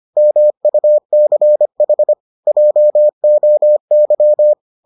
Morse code, heard as a keyed tone, about 25 wpm, 600 Hz. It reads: MUCH JOY